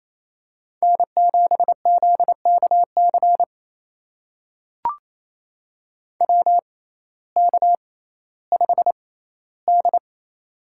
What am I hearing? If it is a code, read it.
Morse code, heard as a keyed tone, about 28 wpm, spaced as for 15 wpm, 700 Hz. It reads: N7ZKC WK5D
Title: USA Call Signs - 6x Character Spacing - 28wpm